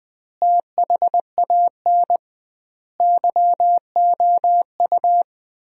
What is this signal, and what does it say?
Morse code, heard as a keyed tone, about 21 wpm, 700 Hz. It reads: THAN YOU
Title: 2x Word-Spacing - Sentences from Top 400 Words - 20wpm